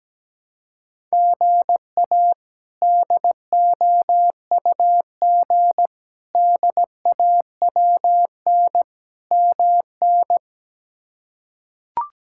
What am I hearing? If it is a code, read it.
Morse code, heard as a keyed tone, about 17 wpm, 700 Hz. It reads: GA DOUG DAWN MN
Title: K1USN SST QSO Practice - 17wpm